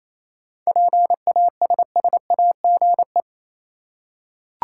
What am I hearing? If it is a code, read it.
Morse code, heard as a keyed tone, about 28 wpm, 700 Hz. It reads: PASSAGE
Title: Seven Letter Words - 28wpm